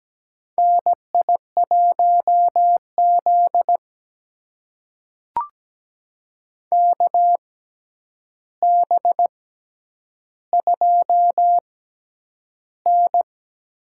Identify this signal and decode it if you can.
Morse code, heard as a keyed tone, about 17 wpm, spaced as for 9 wpm, 700 Hz. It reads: NI1Z KB2N
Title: USA Call Signs - 4 Characters - 6x Character Spacing - 17wpm